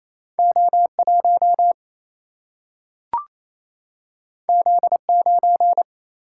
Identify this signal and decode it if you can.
Morse code, heard as a keyed tone, about 28 wpm, 700 Hz. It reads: O1 Z9
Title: Single-Letter Single-Number 28wpm